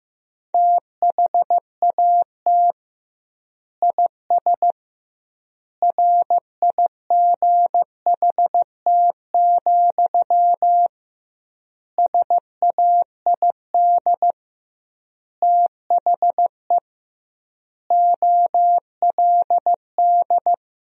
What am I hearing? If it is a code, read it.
Morse code, heard as a keyed tone, about 15 wpm, 700 Hz. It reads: THAT IS RIGHT, SAID THE OLD
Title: 2x Word-Spacing - Sentences from Top 200 Words - 15wpm